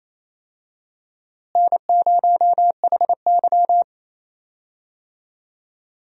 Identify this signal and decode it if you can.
Morse code, heard as a keyed tone, about 28 wpm, 700 Hz. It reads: N0HY
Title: USA Call Signs - 4 Characters - Repeated 3x - 28wpm